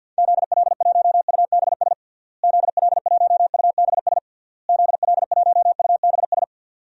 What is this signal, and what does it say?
Morse code, heard as a keyed tone, about 51 wpm, 700 Hz. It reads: ZL1UBS ZL1UBS ZL1UBS
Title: DXCC-100 Call Signs - 6 Characters - Repeated 3x - 50wpm